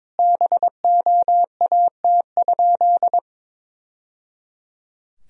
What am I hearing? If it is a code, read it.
Morse code, heard as a keyed tone, about 22 wpm, 700 Hz. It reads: BOAT?
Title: Sentences from Top 1000 Words - Review Part 3 - 22wpm